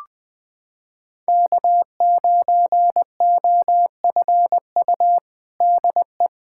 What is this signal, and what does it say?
Morse code, heard as a keyed tone, about 20 wpm, 700 Hz. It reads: K9OFU DE
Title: USA Callsign DE Callsign - 5 Characters - 20wpm